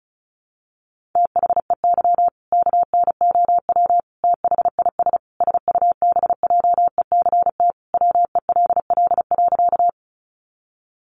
Morse code, heard as a keyed tone, about 34 wpm, 700 Hz. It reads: THEY KNOW THIS SUBJECT WELL.
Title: Sentences from Top 800 Words 35wpm